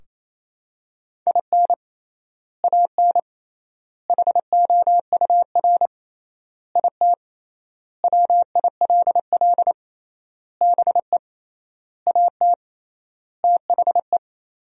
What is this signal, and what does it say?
Morse code, heard as a keyed tone, about 28 wpm, 700 Hz. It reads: IN AN HOUR IT WILL BE AT THE
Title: 3x Word-Spacing - Sentences from Top 400 Words - 28wpm